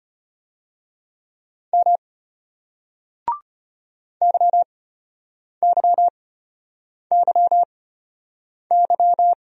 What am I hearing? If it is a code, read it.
Morse code, heard as a keyed tone, about 34 wpm, spaced as for 8 wpm, 700 Hz. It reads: M YYYY